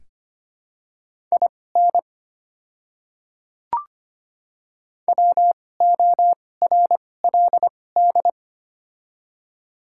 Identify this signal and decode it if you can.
Morse code, heard as a keyed tone, about 25 wpm, spaced as for 18 wpm, 690 Hz. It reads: IN WORLD
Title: 2x Character Spacing - Top 300 Words - 25wpm